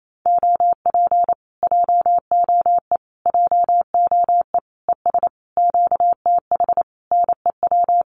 Morse code, heard as a keyed tone, about 28 wpm, 700 Hz. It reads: OP JOE JOE ES QTH NEW
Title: Common QSO Phrases Encore 28wpm